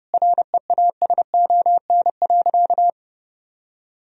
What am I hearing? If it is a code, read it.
Morse code, heard as a keyed tone, about 30 wpm, 700 Hz. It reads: REASON.
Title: Sentences from Top 1000 Words - Review Part 3 - 30wpm